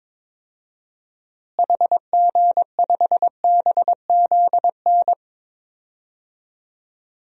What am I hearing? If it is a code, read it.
Morse code, heard as a keyed tone, about 22 wpm, 700 Hz. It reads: HG5BZN
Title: DXCC-100 Call Signs - 6 Characters - Repeated 3x - 22wpm